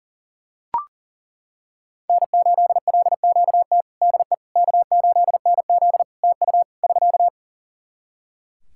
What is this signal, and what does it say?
Morse code, heard as a keyed tone, about 37 wpm, 700 Hz. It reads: N8LQT DE K8NZ TU <SK>
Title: Practice QSO - Exchange 4 - Ending - 40wpm